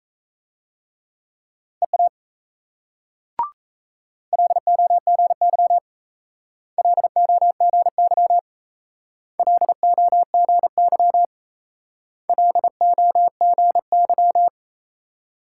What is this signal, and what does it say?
Morse code, heard as a keyed tone, about 35 wpm, 700 Hz. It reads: EA LOGY LOGY LOGY LOGY
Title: Speed-Racing Multi-letter Phonemes - 28wpm